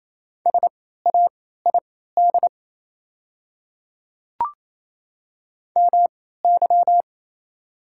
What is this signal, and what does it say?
Morse code, heard as a keyed tone, about 27 wpm, spaced as for 16 wpm, 690 Hz. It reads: SAID MY
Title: 3x Character Spacing - Top 100 Words - 28wpm